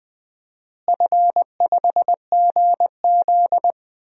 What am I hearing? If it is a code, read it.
Morse code, heard as a keyed tone, about 20 wpm, 700 Hz. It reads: F5GZ
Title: DXCC-100 Call Signs - 4 Characters - Repeated 3x - 20wpm